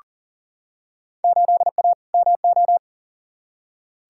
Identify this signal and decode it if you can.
Morse code, heard as a keyed tone, about 40 wpm, 700 Hz. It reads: 8A MO
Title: Field Day Exchanges Encore - 40wpm